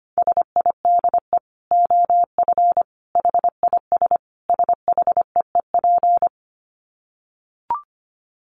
Morse code, heard as a keyed tone, about 25 wpm, 700 Hz. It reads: SIDE OF HIS SHEEP
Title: Dit Challenge 25wpm